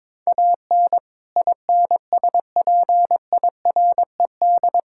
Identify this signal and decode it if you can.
Morse code, heard as a keyed tone, about 22 wpm, 700 Hz. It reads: AN INSPIRED
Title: Pangram Challenge v2 - 22wpm